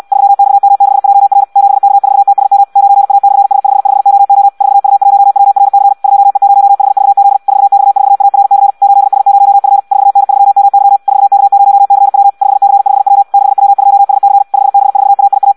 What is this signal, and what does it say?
Morse code, heard as a keyed tone, about 74 wpm, 800 Hz. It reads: 3XM6JR 7K4TSM 9A/S55OO 4U1WRC 2E0HVZ 4X4ARC 8S0F 4N3MY 4K0CW 5C5W 6W2SC 5B4AIA